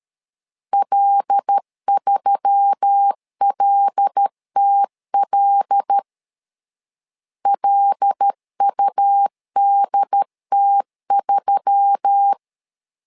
Morse code, heard as a keyed tone, about 13 wpm, 800 Hz. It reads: L3LTL LUDT3
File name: LEZIONI / 5 LBVC / 5 L 12wpm 1913